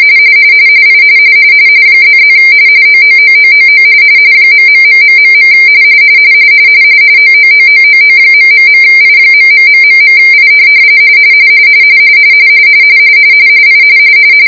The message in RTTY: I ++ OOOOOO ...... OOOOOO...I.....I I